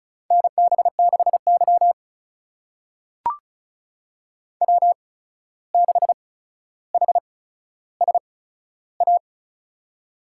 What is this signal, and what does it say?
Morse code, heard as a keyed tone, about 36 wpm, spaced as for 14 wpm, 700 Hz. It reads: NB6Y W6HSA
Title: USA Call Signs - 8x Character Spacing - 35wpm